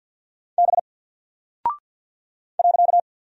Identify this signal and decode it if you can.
Morse code, heard as a keyed tone, about 49 wpm, 700 Hz. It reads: B .
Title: Single Character (Letter:Number:Punctuation:Prosign) 50wpm